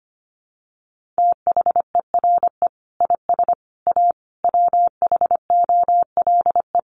Morse code, heard as a keyed tone, about 25 wpm, 700 Hz. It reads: THERE IS A WHOLE
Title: Sentences from Top 1400 Words 25wpm